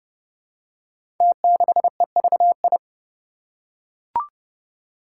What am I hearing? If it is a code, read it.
Morse code, heard as a keyed tone, about 30 wpm, 700 Hz. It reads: T6EVI
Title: Five Challenging Letter-Number Combinations 30wpm